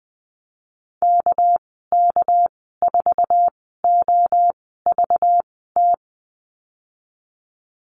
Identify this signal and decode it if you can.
Morse code, heard as a keyed tone, about 20 wpm, spaced as for 14 wpm, 700 Hz. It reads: KK4OVT
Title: 231 Lesson 37g - Letter x - Call Signs 20wpm